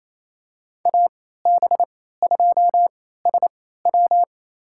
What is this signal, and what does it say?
Morse code, heard as a keyed tone, about 28 wpm, spaced as for 16 wpm, 690 Hz. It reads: AB2SW
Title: USA Call Signs - 3x Character Spacing - 28wpm